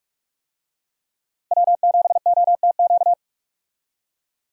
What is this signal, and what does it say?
Morse code, heard as a keyed tone, about 46 wpm, 700 Hz. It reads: W7OTQ
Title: USA Call Signs - 5 Characters - Repeated 2x - 45wpm